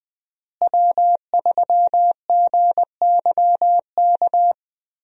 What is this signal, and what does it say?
Morse code, heard as a keyed tone, about 20 wpm, 700 Hz. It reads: W3GYK
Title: USA Call Signs - 5 Characters - Repeated 3x - 20wpm